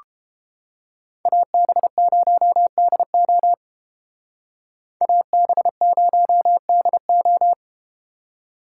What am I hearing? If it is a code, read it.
Morse code, heard as a keyed tone, about 32 wpm, 700 Hz. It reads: AB0DO AB0DO